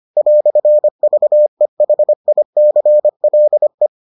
Morse code, heard as a keyed tone, about 25 wpm, 600 Hz. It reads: "VEHICLE